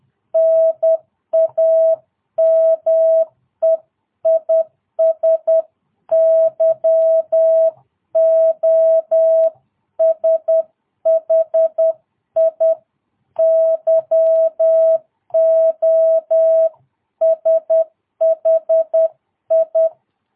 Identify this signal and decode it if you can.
Morse code, heard as a keyed tone, about 10 wpm, 660 Hz. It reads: NAME IS YOSHI YOSHI
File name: jh8cbh / 2022cw